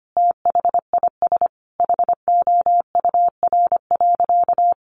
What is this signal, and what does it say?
Morse code, heard as a keyed tone, about 25 wpm, 700 Hz. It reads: THIS HOUR.
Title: Sentences from Top 1100 Words 25wpm